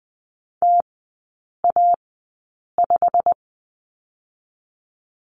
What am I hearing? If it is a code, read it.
Morse code, heard as a keyed tone, about 20 wpm, spaced as for 8 wpm, 700 Hz. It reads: TA5
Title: 172 Lesson 28e - Number 9 - Three Characters 20wpm